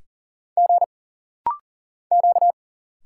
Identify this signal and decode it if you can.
Morse code, heard as a keyed tone, about 40 wpm, 700 Hz. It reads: G Q
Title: Single Letters - Rapid-Fire 40wpm